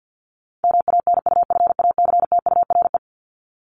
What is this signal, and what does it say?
Morse code, heard as a keyed tone, about 49 wpm, 700 Hz. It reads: MANUFACTURE
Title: ING Suffix - Top 100 Words with Base Words 50wpm